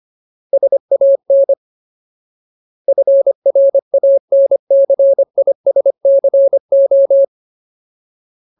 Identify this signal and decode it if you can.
Morse code, heard as a keyed tone, about 25 wpm, 550 Hz. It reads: SAN FRANCISCO
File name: Ultimate Word List / 25WPM Ultimate